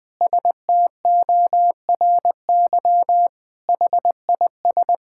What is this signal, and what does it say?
Morse code, heard as a keyed tone, about 20 wpm, 700 Hz. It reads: STORY HIS
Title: Sentences with 5-Word Phrases - 20wpm